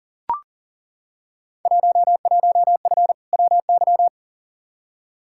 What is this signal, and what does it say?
Morse code, heard as a keyed tone, about 40 wpm, 700 Hz. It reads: 11F WY
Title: Field Day Exchanges Encore - 40wpm